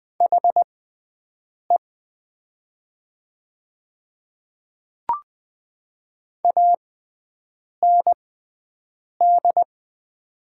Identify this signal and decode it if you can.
Morse code, heard as a keyed tone, about 20 wpm, spaced as for 7 wpm, 700 Hz. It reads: HE AND
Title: 6x Character Spacing - Top 500 Words - 20wpm